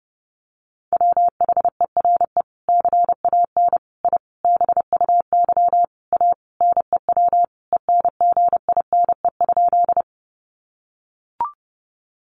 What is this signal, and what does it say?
Morse code, heard as a keyed tone, about 30 wpm, 700 Hz. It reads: WHERE CAN I BUY A NEW ENGINE?
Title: Sentences from Top 1500 Words 30wpm